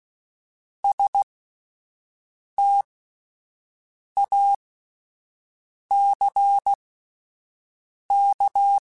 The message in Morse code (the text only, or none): STACK